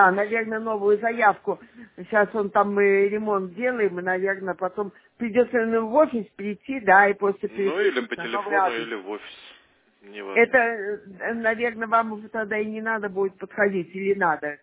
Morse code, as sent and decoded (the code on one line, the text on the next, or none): none